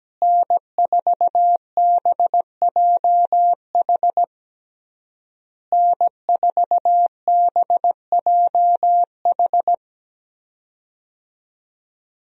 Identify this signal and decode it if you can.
Morse code, heard as a keyed tone, about 17 wpm, 700 Hz. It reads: N4BJH N4BJH
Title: USA Call Signs - 5 Characters - Repeated 3x - 17wpm